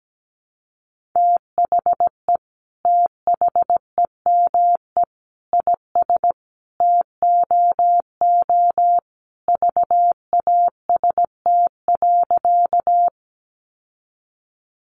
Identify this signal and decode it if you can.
Morse code, heard as a keyed tone, about 17 wpm, 700 Hz. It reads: THE THEME IS TOO VAST.
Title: Sentences from Top 1300 Words 17wpm